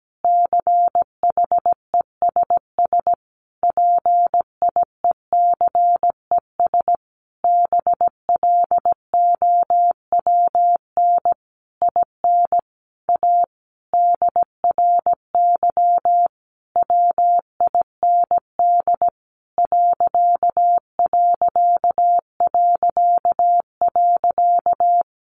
Morse code, heard as a keyed tone, about 17 wpm, 700 Hz. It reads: CHESS PIECES BLOWN IN A DRY WIND ....
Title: A Sound Of Thunder 17wpm